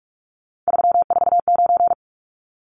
Morse code, heard as a keyed tone, about 44 wpm, 700 Hz. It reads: 349
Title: Numbers - 3-Digit 45wpm